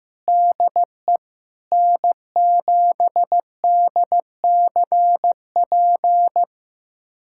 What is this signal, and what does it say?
Morse code, heard as a keyed tone, about 15 wpm, 700 Hz. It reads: DE N7DCP
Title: USA Callsign DE Callsign - 5 Characters - 15wpm